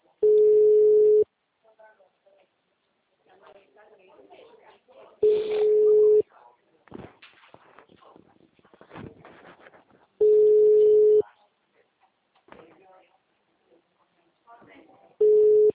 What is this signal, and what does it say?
Ringback tone (Europe, 425 Hz), with speech.
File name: out- -2008- - - .1865